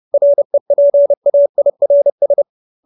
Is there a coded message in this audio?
Yes: Morse code from 0.1 to 2.4 s, heard as a keyed tone, about 30 wpm, 570 Hz. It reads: REPAIRS